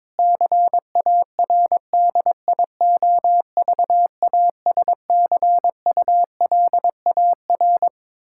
Morse code, heard as a keyed tone, about 22 wpm, 700 Hz. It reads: CARDIOVASCULAR
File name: Difficult Sentences with Hints - 22wpm